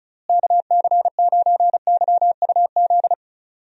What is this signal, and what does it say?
Morse code, heard as a keyed tone, about 36 wpm, 700 Hz. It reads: KC9YUZ
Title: USA Call Signs - 6 Characters - Repeated 2x - 35wpm